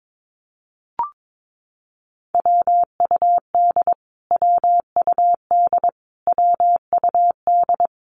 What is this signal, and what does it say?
Morse code, heard as a keyed tone, about 22 wpm, 700 Hz. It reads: WUD WUD WUD
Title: Abbreviations Repeated 3x - 22wpm